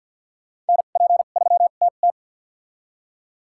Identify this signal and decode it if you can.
Morse code, heard as a keyed tone, about 52 wpm, spaced as for 36 wpm, 700 Hz. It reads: NP3TT
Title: USA Call Signs - 5 Characters - 2x Character Spacing - 50wpm